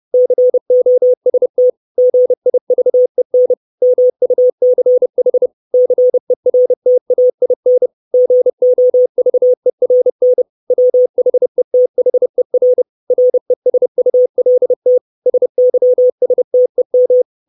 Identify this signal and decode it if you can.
Morse code, heard as a keyed tone, about 30 wpm, 500 Hz. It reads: COST GIVEN MUCH CERTAIN GOVERN WHETHER RESULT SYSTEM
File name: 632 Words 30WPM